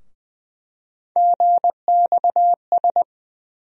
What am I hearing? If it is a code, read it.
Morse code, heard as a keyed tone, about 21 wpm, 700 Hz. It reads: GXS
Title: USA Call Sign Postfixes - 3 Letters 20wpm